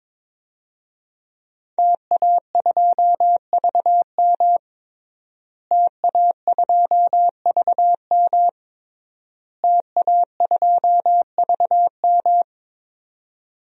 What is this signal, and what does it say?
Morse code, heard as a keyed tone, about 22 wpm, 700 Hz. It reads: TA2VM TA2VM TA2VM
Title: DXCC-100 Call Signs - 5 Characters - Repeated 3x - 22wpm